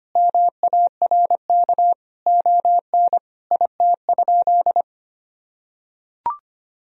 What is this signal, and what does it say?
Morse code, heard as a keyed tone, about 25 wpm, 700 Hz. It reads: MARK ON IT?
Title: Sentences from Top 1000 Words - Review Part 1 - 25wpm